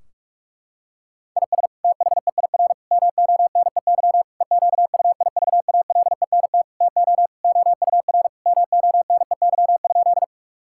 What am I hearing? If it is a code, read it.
Morse code, heard as a keyed tone, about 41 wpm, 700 Hz. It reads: IS THEIR MONEY EQUIVALENT TO OUR MONEY?
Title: Sentences from Top 1600 Words 45wpm